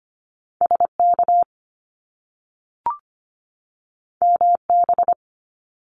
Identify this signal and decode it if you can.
Morse code, heard as a keyed tone, about 25 wpm, 700 Hz. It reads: SK MB
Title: Canadian Provinces and Territories 25wpm